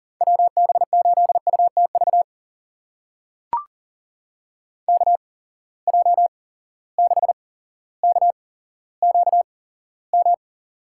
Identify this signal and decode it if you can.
Morse code, heard as a keyed tone, about 40 wpm, spaced as for 17 wpm, 700 Hz. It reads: WB8UTV KJ6KQM
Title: USA Call Signs - 6 Characters - 8x Character Spacing - 40wpm